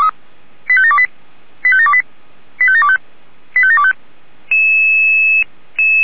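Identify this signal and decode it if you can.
Five-tone selective call (ZVEI), heard as quick runs of five tones, tones of about 70 ms. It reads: 85728 85728 85725 85725
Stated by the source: ZVEI five-tone selective call received on SDR.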